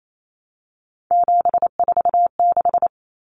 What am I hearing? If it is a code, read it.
Morse code, heard as a keyed tone, about 28 wpm, 700 Hz. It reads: 746
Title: Common Cut Numbers - 3-Digit 28wpm